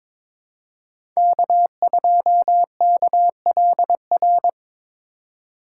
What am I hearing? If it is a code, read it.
Morse code, heard as a keyed tone, about 21 wpm, 700 Hz. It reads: K2KLR
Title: USA Call Signs - 4x Character Spacing - 22wpm